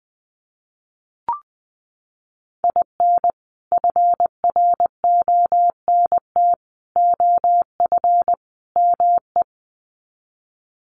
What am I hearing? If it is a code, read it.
Morse code, heard as a keyed tone, about 20 wpm, 700 Hz. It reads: IN FRONT OF ME
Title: Sets of 4 Words 20wpm